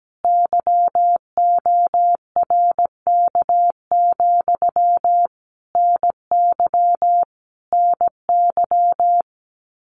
Morse code, heard as a keyed tone, about 17 wpm, 700 Hz. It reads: YORK, NY NY
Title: Common QSO Phrases Encore 17wpm